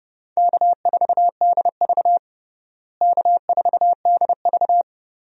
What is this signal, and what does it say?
Morse code, heard as a keyed tone, about 30 wpm, 700 Hz. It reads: K4DV K4DV
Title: USA Call Signs - 4 Characters - Repeated 2x - 30wpm